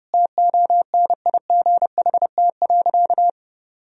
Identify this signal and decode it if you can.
Morse code, heard as a keyed tone, about 30 wpm, 700 Hz. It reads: TONIGHT.
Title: Sentences from Top 1000 Words - Review Part 3 - 30wpm